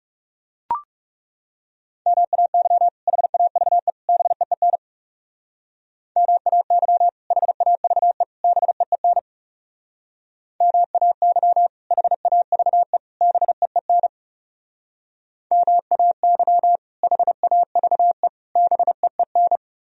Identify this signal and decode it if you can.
Morse code, heard as a keyed tone, about 39 wpm, 700 Hz. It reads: MAY HAVE BEEN MAY HAVE BEEN MAY HAVE BEEN MAY HAVE BEEN